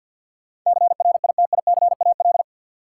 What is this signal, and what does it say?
Morse code, heard as a keyed tone, about 46 wpm, 700 Hz. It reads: CRITICAL
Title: Sentences from Top 1100 Words - New Word Drill Alt - 50wpm